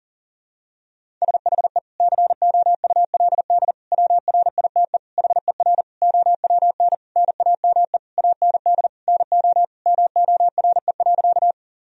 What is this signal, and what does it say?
Morse code, heard as a keyed tone, about 40 wpm, 700 Hz. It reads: SHE COULD WRITE HER OWN NAME AND NO MORE.
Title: Sentences from Top 1000 Words - Review Part 1 - 40wpm